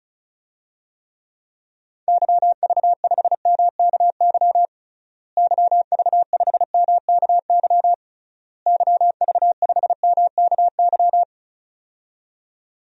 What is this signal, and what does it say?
Morse code, heard as a keyed tone, about 36 wpm, 700 Hz. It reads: YV5MKY YV5MKY YV5MKY
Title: DXCC-100 Call Signs - 6 Characters - Repeated 3x - 35wpm